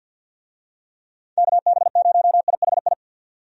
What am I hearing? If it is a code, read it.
Morse code, heard as a keyed tone, about 51 wpm, 700 Hz. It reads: KB0IHI
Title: USA Call Signs - 6 Characters - 50wpm